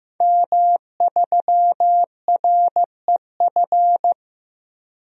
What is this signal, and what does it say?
Morse code, heard as a keyed tone, about 15 wpm, 700 Hz. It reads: M3REF
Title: DXCC-100 Call Signs - 5 Characters - 15wpm